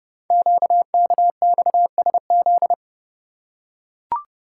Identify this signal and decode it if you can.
Morse code, heard as a keyed tone, about 28 wpm, 700 Hz. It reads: QKXSZ
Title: Code Groups - 5 Letters - 30wpm